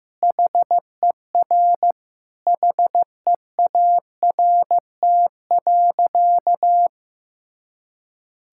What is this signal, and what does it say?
Morse code, heard as a keyed tone, about 15 wpm, 700 Hz. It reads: HER HEART.
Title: Sentences from Top 1600 Words 15wpm